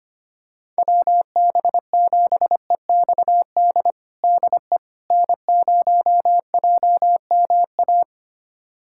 Morse code, heard as a keyed tone, about 25 wpm, 700 Hz. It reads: WB7EXD DE N0JMA
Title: USA Callsign DE Callsign 25wpm